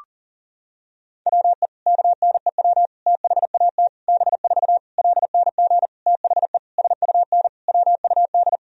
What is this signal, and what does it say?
Morse code, heard as a keyed tone, about 40 wpm, 700 Hz. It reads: WE KNEW THAT B4 LNG THE SUN WUD
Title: Sentences from Top 400 Words and Abbreviations 40wpm